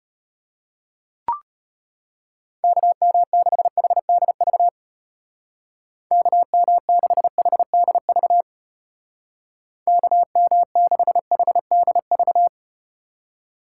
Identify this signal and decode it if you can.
Morse code, heard as a keyed tone, about 34 wpm, 700 Hz. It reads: KM6HDV KM6HDV KM6HDV